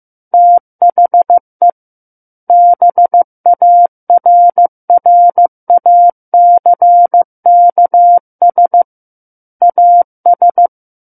Morse code, heard as a keyed tone, about 15 wpm, 700 Hz. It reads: THE BARRACKS AS
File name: Book One / 06-Chapter 6 - 15 WPM